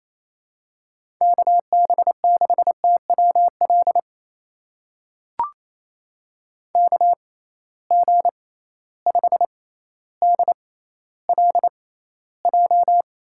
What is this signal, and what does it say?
Morse code, heard as a keyed tone, about 28 wpm, spaced as for 14 wpm, 700 Hz. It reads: KB6TWL KG5DLJ
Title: USA Call Signs - 6 Characters - 6x Character Spacing - 28wpm